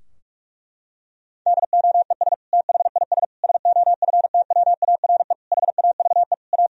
Morse code, heard as a keyed tone, about 43 wpm, 700 Hz. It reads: DOES THIS SOFTWARE HAVE A
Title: Sentences from Top 1000 Words - Review Part 3 - 45wpm